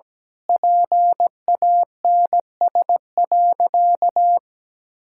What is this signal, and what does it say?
Morse code, heard as a keyed tone, about 17 wpm, 700 Hz. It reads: PANS.
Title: Pangram Challenge v2 - 17wpm